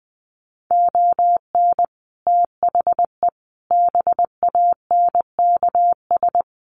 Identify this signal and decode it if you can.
Morse code, heard as a keyed tone, about 20 wpm, 700 Hz. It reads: ON THE BANKS
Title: Sets of 5 Words Encore2 20wpm